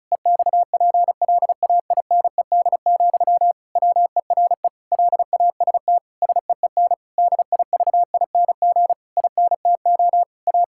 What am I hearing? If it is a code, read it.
Morse code, heard as a keyed tone, about 34 wpm, 700 Hz. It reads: EXPLAINED, WERE LAST SEEN DIVING INTO A
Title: Pangram Challenge v2 - 35wpm